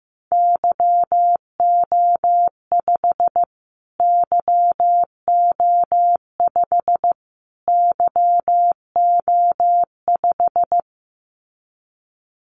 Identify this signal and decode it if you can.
Morse code, heard as a keyed tone, about 15 wpm, 700 Hz. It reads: YO5 YO5 YO5
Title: Callsign Prefixes for Top 25 DXCC - Repeated 3x - 15wpm